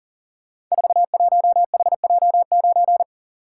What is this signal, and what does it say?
Morse code, heard as a keyed tone, about 42 wpm, 700 Hz. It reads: 41HJ9
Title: Code Groups - 5 Letter:Numbers - 40wpm